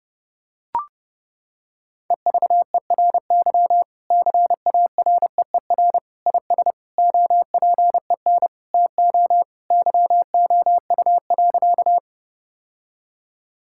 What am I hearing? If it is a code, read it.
Morse code, heard as a keyed tone, about 30 wpm, 700 Hz. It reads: EVERY CAREER IS OPEN TO YOU.
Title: Sentences from Top 1000 Words - Review Part 6 - 30wpm